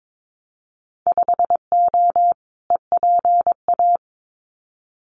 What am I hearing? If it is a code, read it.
Morse code, heard as a keyed tone, about 22 wpm, 700 Hz. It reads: 5O EPA
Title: Winter Field Day Exchanges Encore 22wpm